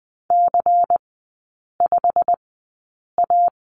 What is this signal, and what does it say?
Morse code, heard as a keyed tone, about 19 wpm, spaced as for 8 wpm, 690 Hz. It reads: C5A
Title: 166 Lesson 27e - Number 7 - Three Characters 20wpm